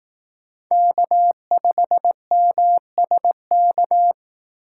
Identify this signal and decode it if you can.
Morse code, heard as a keyed tone, about 19 wpm, 700 Hz. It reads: K5MSK
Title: Speed Racing - USA Call Signs 15wpm